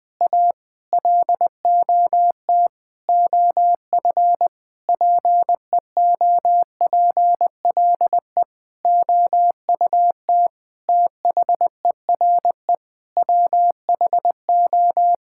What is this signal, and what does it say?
Morse code, heard as a keyed tone, about 21 wpm, 700 Hz. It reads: A LOT OF PEOPLE OUT THERE WHO
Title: Sentences with 8-Word Phrases - 20wpm